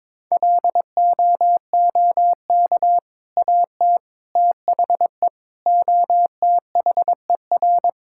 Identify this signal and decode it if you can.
Morse code, heard as a keyed tone, about 22 wpm, 700 Hz. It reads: LOOK AT THE OTHER
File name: practice / 22wpm / Sentences with 2-Word Phrases - 22wpm